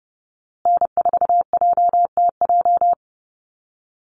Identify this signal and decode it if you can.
Morse code, heard as a keyed tone, about 30 wpm, 700 Hz. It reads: N4JTJ
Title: USA Call Signs Repeated 3x 30wpm